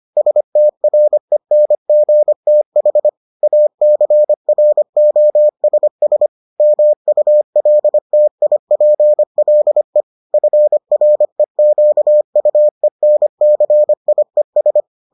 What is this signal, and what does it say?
Morse code, heard as a keyed tone, about 25 wpm, 600 Hz. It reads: STRENGTH ACROSS MULTIPLE FREQUENCIES